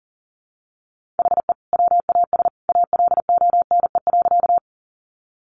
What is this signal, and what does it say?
Morse code, heard as a keyed tone, about 40 wpm, 700 Hz. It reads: HE WAS ALONE.
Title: Sentences from Top 1000 Words 40wpm